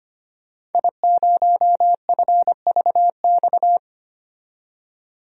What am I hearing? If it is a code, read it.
Morse code, heard as a keyed tone, about 25 wpm, 700 Hz. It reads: I0FVX
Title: DXCC-100 Call Signs - 5 Characters - 25wpm